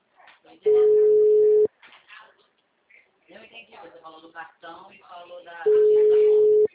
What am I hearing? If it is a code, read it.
Ringback tone (Europe, 425 Hz), with speech.
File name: out- -2014- - - .55399